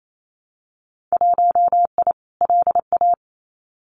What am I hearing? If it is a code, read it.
Morse code, heard as a keyed tone, about 28 wpm, 700 Hz. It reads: 1I LA
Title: Winter Field Day Exchanges Encore 28wpm